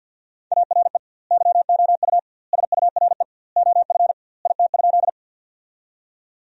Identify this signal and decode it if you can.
Morse code, heard as a keyed tone, about 50 wpm, 700 Hz. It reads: ARE YOU SURE OF IT?
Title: 2x Word-Spacing - Sentences from Top 300 Words - 50wpm